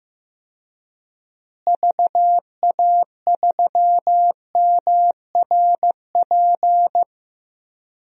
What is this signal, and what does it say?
Morse code, heard as a keyed tone, about 15 wpm, 700 Hz. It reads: VA3MRP
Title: DXCC-100 Call Signs - 6 Characters - Repeated 2x - 15wpm